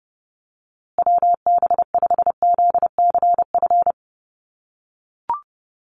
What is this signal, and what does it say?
Morse code, heard as a keyed tone, about 30 wpm, 700 Hz. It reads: WB5ZCF
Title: USA Call Signs Repeated 3x 30wpm